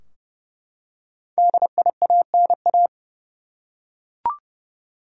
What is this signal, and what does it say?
Morse code, heard as a keyed tone, about 30 wpm, 700 Hz. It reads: DIANA
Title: USA Common Names - Max 5 Letters - 30wpm